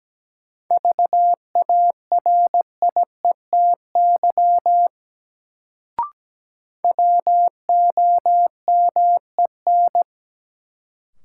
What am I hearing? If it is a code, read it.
Morse code, heard as a keyed tone, about 17 wpm, 700 Hz. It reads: VARIETY WOMEN
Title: Sets of 3 Words - New Word Drill - 17wpm